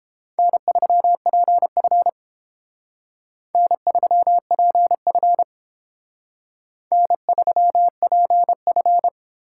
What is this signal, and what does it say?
Morse code, heard as a keyed tone, about 30 wpm, 700 Hz. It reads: N3PF N3PF N3PF